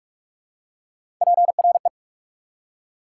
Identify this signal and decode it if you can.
Morse code, heard as a keyed tone, about 46 wpm, 690 Hz. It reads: PRE
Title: Prefixes and Suffixes 45wpm